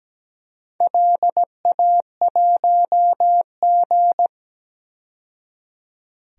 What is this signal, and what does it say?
Morse code, heard as a keyed tone, about 17 wpm, 700 Hz. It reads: LA1G
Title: DXCC-100 Call Signs - 4 Characters - 17wpm